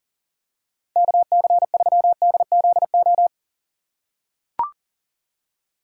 Morse code, heard as a keyed tone, about 38 wpm, 700 Hz. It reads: KC3DZO
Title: USA Call Signs - 6 Characters - 4x Character Spacing - 40wpm